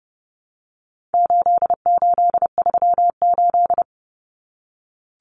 Morse code, heard as a keyed tone, about 30 wpm, 700 Hz. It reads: 8838
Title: Numbers - 4-Digit 30wpm